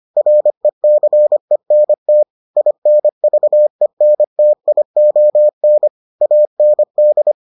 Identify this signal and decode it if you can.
Morse code, heard as a keyed tone, about 25 wpm, 600 Hz. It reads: RECENT INVENTION AND